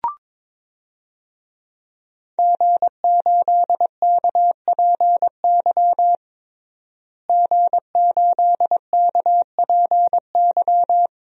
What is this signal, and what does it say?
Morse code, heard as a keyed tone, about 22 wpm, 700 Hz. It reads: G8KPY G8KPY
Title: DXCC-100 Call Signs - 5 Characters - Repeated 2x - 22wpm